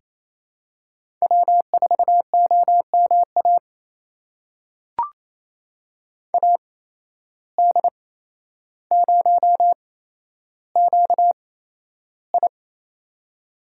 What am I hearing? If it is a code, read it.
Morse code, heard as a keyed tone, about 28 wpm, spaced as for 12 wpm, 700 Hz. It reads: W4OMA AD0QI
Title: USA Call Signs - 8x Character Spacing - 28wpm